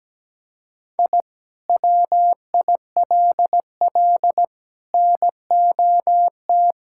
Morse code, heard as a keyed tone, about 17 wpm, 700 Hz. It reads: I WILL NOT
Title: Sentences from Top 1000 Words - Review Part 5 - 17wpm